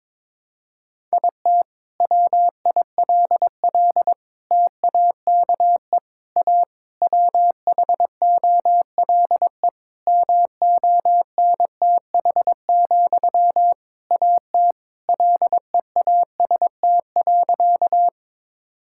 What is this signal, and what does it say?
Morse code, heard as a keyed tone, about 22 wpm, 700 Hz. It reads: IT WILL TAKE A WHOLE MONTH, AT LEAST.
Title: Sentences from Top 1000 Words - Review Part 5 - 22wpm